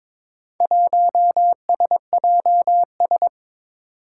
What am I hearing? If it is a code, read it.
Morse code, heard as a keyed tone, about 22 wpm, 700 Hz. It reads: 1SJS
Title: Four Challenging Letter-Number Combinations 22wpm